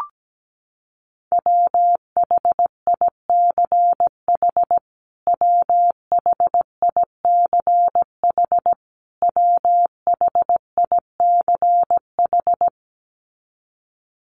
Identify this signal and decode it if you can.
Morse code, heard as a keyed tone, about 17 wpm, 700 Hz. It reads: WHICH WHICH WHICH
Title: Top 300 Words Repeated 17wpm